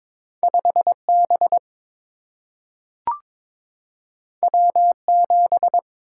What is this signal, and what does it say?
Morse code, heard as a keyed tone, about 22 wpm, 700 Hz. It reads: 5B W7
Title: Single-Letter Single-Number 22wpm